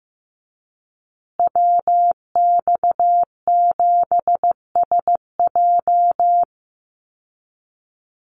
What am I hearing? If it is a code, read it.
Morse code, heard as a keyed tone, about 15 wpm, 700 Hz. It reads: WX7SJ
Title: Call Signs Repeated 2x - 15wpm